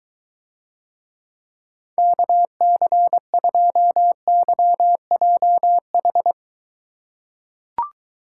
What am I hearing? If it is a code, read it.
Morse code, heard as a keyed tone, about 23 wpm, 700 Hz. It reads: KC2YJH